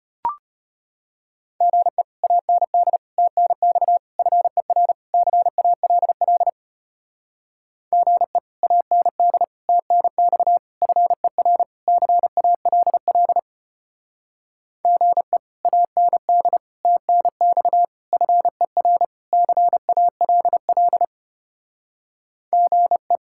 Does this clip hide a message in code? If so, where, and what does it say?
Yes: Morse code from 1.6 to 23.2 s, heard as a keyed tone, about 32 wpm, 700 Hz. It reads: GE AND TNX FER CALL GE AND TNX FER CALL GE AND TNX FER CALL GE